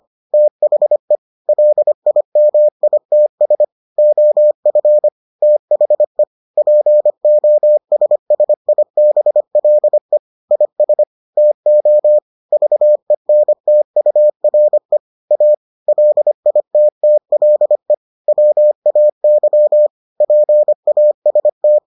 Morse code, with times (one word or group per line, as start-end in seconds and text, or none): 0.3-1.2 THE
1.5-3.6 LIMITS
4.0-5.1 OF
5.4-6.2 THE
6.6-10.2 POSSIBLE
10.5-11.0 IS
11.4-12.2 TO
12.5-15.0 VENTURE
15.3-15.5 A
15.9-17.9 LITTLE
18.3-19.9 WAY
20.2-21.8 PAST